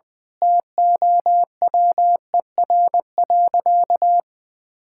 Morse code, heard as a keyed tone, about 20 wpm, 700 Hz. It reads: TOWER.
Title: Pangram Challenge v2 - 20wpm